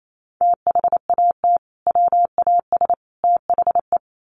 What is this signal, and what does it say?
Morse code, heard as a keyed tone, about 28 wpm, 700 Hz. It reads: THAT WAS THE
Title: Sets of 3 Words 28wpm